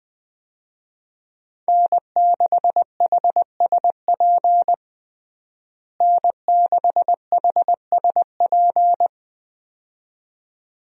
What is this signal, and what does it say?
Morse code, heard as a keyed tone, about 21 wpm, 700 Hz. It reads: N6HSP N6HSP
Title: USA Call Signs - 5 Characters - Repeated 2x - 20wpm